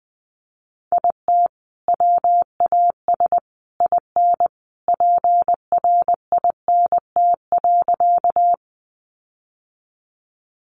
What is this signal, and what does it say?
Morse code, heard as a keyed tone, about 20 wpm, 700 Hz. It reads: IT WAS IN PRINT.
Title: Sentences from Top 1400 Words 20wpm